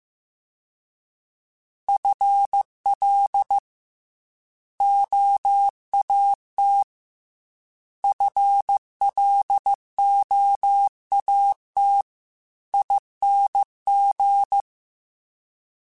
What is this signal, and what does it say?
Morse code, heard as a keyed tone, about 15 wpm, 800 Hz. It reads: FL OAT FLOAT ING
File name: FL / FL mp3s / Fast FL @13